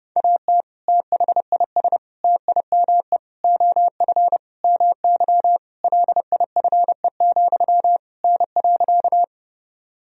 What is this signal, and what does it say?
Morse code, heard as a keyed tone, about 30 wpm, 700 Hz. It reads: AT THIS TIME OF MY LIFE, N.
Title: Sentences from Top 300 Words and Abbreviations 30wpm